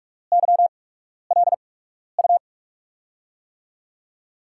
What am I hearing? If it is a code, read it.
Morse code, heard as a keyed tone, about 44 wpm, spaced as for 12 wpm, 700 Hz. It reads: YLU
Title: USA Call Signs - 8x Character Spacing - 45wpm